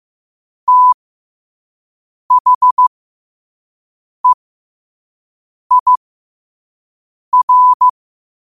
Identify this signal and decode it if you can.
Morse code, heard as a keyed tone, about 15 wpm, spaced as for 5 wpm, 1000 Hz. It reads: THEIR
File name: qotdfiles / 5wpm